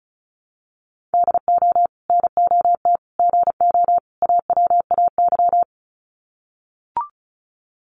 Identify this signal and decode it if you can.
Morse code, heard as a keyed tone, about 34 wpm, 700 Hz. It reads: DO NOT GO AWAY
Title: Sets of 4 Words 35wpm